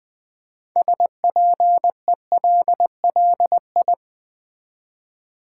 Morse code, heard as a keyed tone, about 20 wpm, 700 Hz. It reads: SPELLI
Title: Build A Word - 2 Letters At A Time - 8-Letter Words - 20wpm